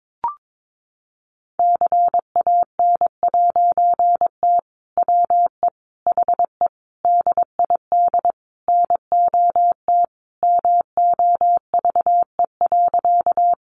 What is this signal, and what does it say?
Morse code, heard as a keyed tone, about 22 wpm, 700 Hz. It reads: CAN'T WE HE DID NOT MOVE.
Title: A Sound Of Thunder 22wpm